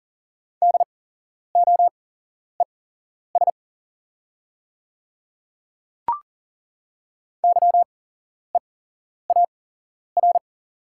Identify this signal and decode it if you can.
Morse code, heard as a keyed tone, about 40 wpm, spaced as for 11 wpm, 700 Hz. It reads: DOES YEAR
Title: 8x Character Spacing - Top 500 Words - 40wpm